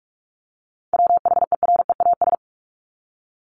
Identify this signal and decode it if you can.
Morse code, heard as a keyed tone, about 44 wpm, 700 Hz. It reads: WHEREAS
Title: Top 1501-1750 Words 45wpm